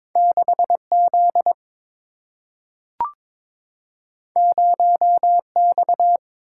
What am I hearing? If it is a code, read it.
Morse code, heard as a keyed tone, about 22 wpm, 700 Hz. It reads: 6Z 0X
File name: practice / 22wpm / Two Letter-Number Combinations - 22wpm